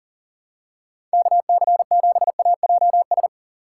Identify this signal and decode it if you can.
Morse code, heard as a keyed tone, about 38 wpm, 700 Hz. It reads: KC7AJS
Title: USA Call Signs - 6 Characters - 3x Character Spacing - 40wpm